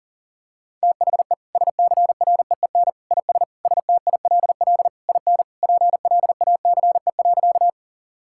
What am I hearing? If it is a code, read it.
Morse code, heard as a keyed tone, about 37 wpm, 700 Hz. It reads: THE SCREEN IS STILL IN PLACE.
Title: Sentences from Top 1000 Words - Review Part 6 - 40wpm